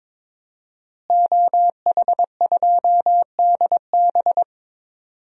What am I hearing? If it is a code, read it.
Morse code, heard as a keyed tone, about 22 wpm, 700 Hz. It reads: OH2DB
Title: DXCC-100 Call Signs - 5 Characters - Repeated 2x - 22wpm